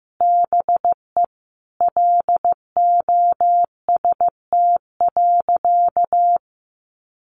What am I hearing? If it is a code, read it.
Morse code, heard as a keyed tone, about 15 wpm, 700 Hz. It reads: BE LOST.
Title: Sentences from Top 1400 Words 15wpm